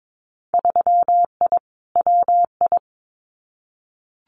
Morse code, heard as a keyed tone, about 22 wpm, 700 Hz. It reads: 3I WI
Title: Winter Field Day Exchanges 22wpm